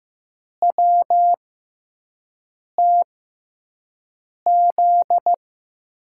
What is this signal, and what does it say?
Morse code, heard as a keyed tone, about 15 wpm, spaced as for 5 wpm, 700 Hz. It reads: WTZ
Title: USA Call Signs - 6 Characters - 6x Character Spacing - 15wpm